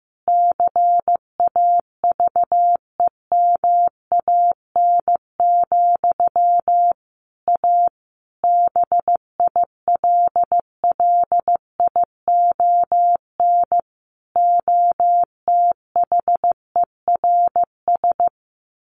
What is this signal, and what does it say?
Morse code, heard as a keyed tone, about 15 wpm, 700 Hz. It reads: CAVEMAN, A BILLION OTHERS
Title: A Sound Of Thunder 15wpm